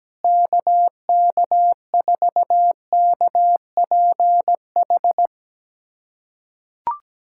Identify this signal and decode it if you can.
Morse code, heard as a keyed tone, about 17 wpm, 700 Hz. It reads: KK4KPH
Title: USA Callsign DE Callsign 17wpm